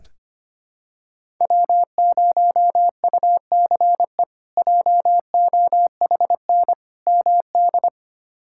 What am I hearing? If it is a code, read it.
Morse code, heard as a keyed tone, about 25 wpm, 700 Hz. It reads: W0UCE JOHN MD
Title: K1USN SST QSO Practice - 25wpm